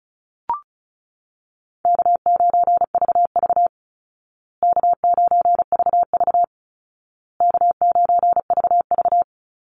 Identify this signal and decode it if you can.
Morse code, heard as a keyed tone, about 34 wpm, 700 Hz. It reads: K9VV K9VV K9VV
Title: Easy Call Signs Repeated 3x - 35wpm